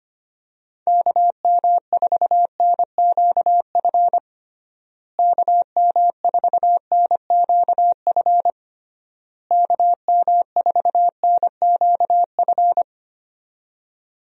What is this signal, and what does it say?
Morse code, heard as a keyed tone, about 25 wpm, 700 Hz. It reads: KM4NQF KM4NQF KM4NQF
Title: USA Call Signs - 6 Characters - Repeated 3x - 25wpm